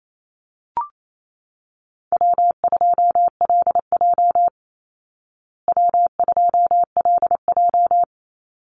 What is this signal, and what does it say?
Morse code, heard as a keyed tone, about 28 wpm, 700 Hz. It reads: W2LJ W2LJ
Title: Easy Call Signs Repeated 3x - 28wpm